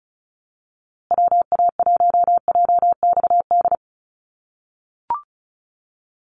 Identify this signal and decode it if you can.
Morse code, heard as a keyed tone, about 34 wpm, 700 Hz. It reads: WA1JXD
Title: USA Call Signs Repeated 2x 35wpm